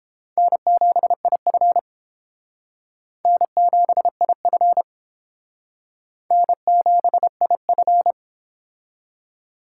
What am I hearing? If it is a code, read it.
Morse code, heard as a keyed tone, about 29 wpm, 700 Hz. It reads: N7IF N7IF N7IF